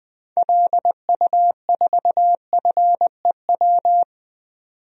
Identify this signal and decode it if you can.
Morse code, heard as a keyed tone, about 20 wpm, 700 Hz. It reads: LU4FEW
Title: DXCC-100 Call Signs - 6 Characters - 20wpm